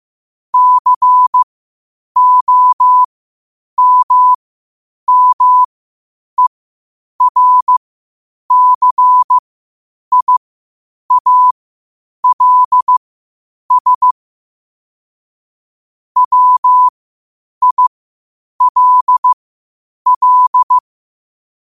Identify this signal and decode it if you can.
Morse code, heard as a keyed tone, about 15 wpm, spaced as for 8 wpm, 1000 Hz. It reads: COMMERCIALS WILL
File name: qotdfiles / 8wpm